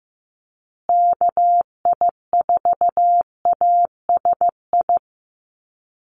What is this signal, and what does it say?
Morse code, heard as a keyed tone, about 15 wpm, 700 Hz. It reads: KI4ASI
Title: USA Call Signs Repeated 3x 15wpm